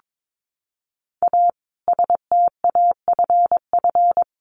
Morse code, heard as a keyed tone, about 22 wpm, 700 Hz. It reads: A STAFF
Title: Additional Easy Phrases 22wpm